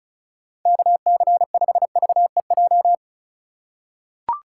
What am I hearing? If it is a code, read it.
Morse code, heard as a keyed tone, about 36 wpm, 700 Hz. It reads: KC5VEJ
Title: USA Call Signs - 6 Characters - 4x Character Spacing - 35wpm